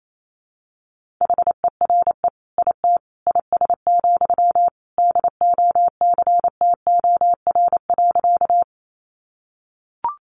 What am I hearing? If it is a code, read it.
Morse code, heard as a keyed tone, about 28 wpm, 700 Hz. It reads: HERE IT IS, DOCTOR.
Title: Sentences from Top 1500 Words 28wpm